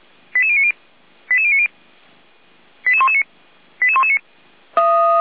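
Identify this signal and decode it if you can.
Five-tone selective call (ZVEI), heard as quick runs of five tones, tones of about 70 ms. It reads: 88090 88090 88119 88119
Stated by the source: ZVEI five-tone selective call received on an SDR.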